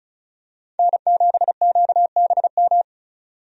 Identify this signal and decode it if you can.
Morse code, heard as a keyed tone, about 36 wpm, 700 Hz. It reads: N7QBM
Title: USA Call Signs - 5 Characters - 35wpm